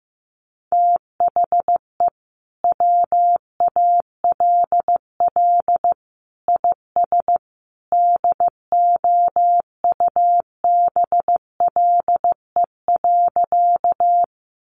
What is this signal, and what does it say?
Morse code, heard as a keyed tone, about 14 wpm, 700 Hz. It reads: THE WALL IS DOUBLE.
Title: Sentences from Top 1400 Words 15wpm